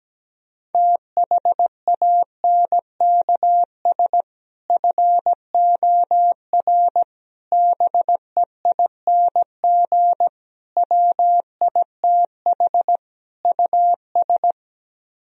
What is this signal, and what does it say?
Morse code, heard as a keyed tone, about 17 wpm, 700 Hz. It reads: THANKS FOR BEING WITH US
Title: Speed-Racing - Sets of 5 Words 17wpm